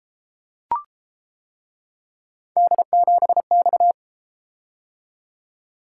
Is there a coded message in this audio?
Yes: Morse code from 2.6 to 3.9 s, heard as a keyed tone, about 31 wpm, 700 Hz. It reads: D7X